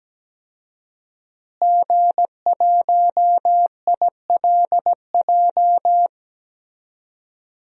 Morse code, heard as a keyed tone, about 17 wpm, 700 Hz. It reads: G1ILJ
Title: DXCC-100 Call Signs - 5 Characters - Repeated 2x - 17wpm